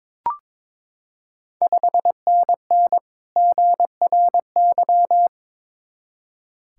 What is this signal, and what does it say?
Morse code, heard as a keyed tone, about 21 wpm, 700 Hz. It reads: 5NN GRY
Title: Ontario QSO Party Exchange 22wpm